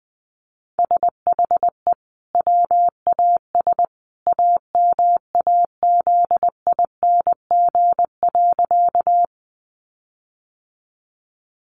Morse code, heard as a keyed tone, about 20 wpm, 700 Hz. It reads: SHE WAS AMAZING.
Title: Sentences from Top 1500 Words 20wpm